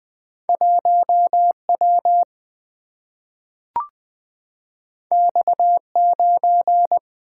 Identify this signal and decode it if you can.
Morse code, heard as a keyed tone, about 20 wpm, 700 Hz. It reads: 1W X9
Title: Single-Letter Single-Number 20wpm